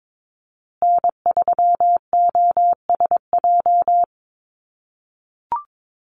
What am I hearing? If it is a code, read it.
Morse code, heard as a keyed tone, about 22 wpm, 700 Hz. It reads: N3OSJ
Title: USA Call Signs Repeated 2x 22wpm